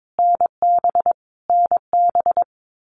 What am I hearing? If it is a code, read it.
Morse code, heard as a keyed tone, about 22 wpm, 700 Hz. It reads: NB NB
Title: Canadian Provinces and Territories - Repeated 3x - 22wpm